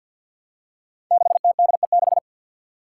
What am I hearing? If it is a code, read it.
Morse code, heard as a keyed tone, about 47 wpm, 700 Hz. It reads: 6TDE6
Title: Five Challenging Letter-Number Combinations 50wpm